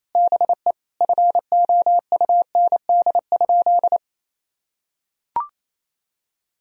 Morse code, heard as a keyed tone, about 28 wpm, 700 Hz. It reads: BE FOUND?
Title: Sentences from Top 300 Words and Abbreviations 28wpm